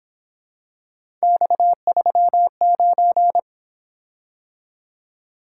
Morse code, heard as a keyed tone, about 26 wpm, 700 Hz. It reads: X39
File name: Speed-Racing - Three Letter-Number Combinations - 22wpm